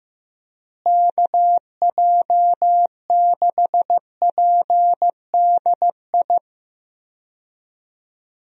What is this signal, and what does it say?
Morse code, heard as a keyed tone, about 15 wpm, 700 Hz. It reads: KJ6PDI
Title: USA Call Signs - 6 Characters - Repeated 3x - 15wpm